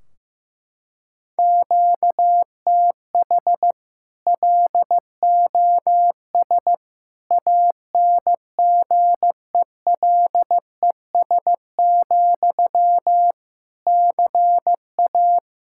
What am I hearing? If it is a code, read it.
Morse code, heard as a keyed tone, about 15 wpm, 700 Hz. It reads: QTH LOS ANGELES, CA
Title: Practice QSO - Exchange 1 - Beginning - 15wpm